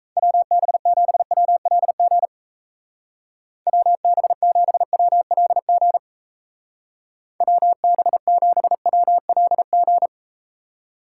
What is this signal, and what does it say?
Morse code, heard as a keyed tone, about 37 wpm, 700 Hz. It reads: WB7WLG WB7WLG WB7WLG